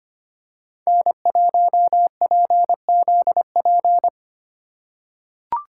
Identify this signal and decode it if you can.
Morse code, heard as a keyed tone, about 25 wpm, 700 Hz. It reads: N1PZP
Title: USA Call Signs - 5 Characters - 2x Character Spacing - 25wpm